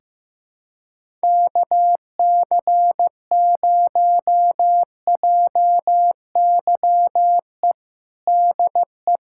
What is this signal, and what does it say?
Morse code, heard as a keyed tone, about 15 wpm, 700 Hz. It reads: KC0JYE DE
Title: USA Callsign DE Callsign - 6 Characters - 15wpm